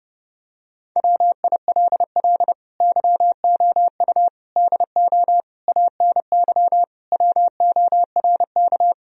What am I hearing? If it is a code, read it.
Morse code, heard as a keyed tone, about 30 wpm, 700 Hz. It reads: WILL YOU DO ANY WORK
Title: Sentences from Top 1000 Words - Review Part 3 - 30wpm